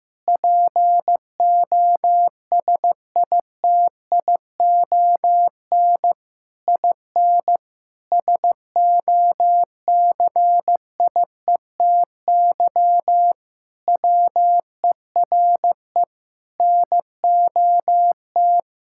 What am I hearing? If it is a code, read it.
Morse code, heard as a keyed tone, about 16 wpm, 700 Hz. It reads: POSITION IN SOCIETY WERE NOT
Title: Sentences from Top 1000 Words - Review Part 5 - 15wpm